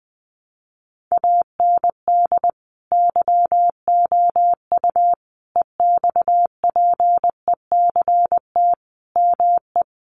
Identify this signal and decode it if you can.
Morse code, heard as a keyed tone, about 20 wpm, 700 Hz. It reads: AND YOU EXPECT ME
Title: Sentences from Top 1100 Words 20wpm